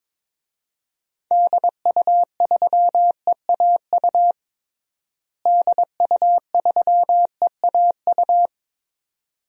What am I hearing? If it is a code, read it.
Morse code, heard as a keyed tone, about 22 wpm, 700 Hz. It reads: DU3EAU DU3EAU
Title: DXCC-100 Call Signs - 6 Characters - Repeated 2x - 22wpm